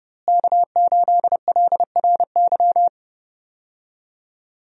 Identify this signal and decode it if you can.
Morse code, heard as a keyed tone, about 30 wpm, 700 Hz. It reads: K8LRY
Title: USA Call Signs - 5 Characters - Repeated 3x - 30wpm